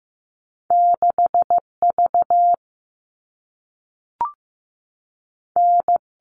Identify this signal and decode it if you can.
Morse code, heard as a keyed tone, about 15 wpm, 700 Hz. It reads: 6 V N
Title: All Cut Numbers - 2-Digit 15wpm